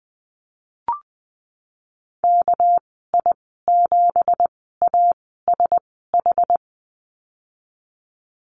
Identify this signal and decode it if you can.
Morse code, heard as a keyed tone, about 20 wpm, spaced as for 14 wpm, 700 Hz. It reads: KI7ASH
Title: 200 Lesson 32g - Letter k - Call Signs 20wpm